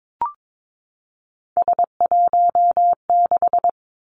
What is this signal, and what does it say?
Morse code, heard as a keyed tone, about 22 wpm, 700 Hz. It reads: S16
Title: Three Challenging Letter-Number Combinations 22wpm